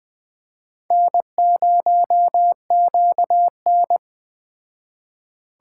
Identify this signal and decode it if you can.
Morse code, heard as a keyed tone, about 20 wpm, 700 Hz. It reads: N0QN
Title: Speed Racing - USA 1x2 Call Signs 17wpm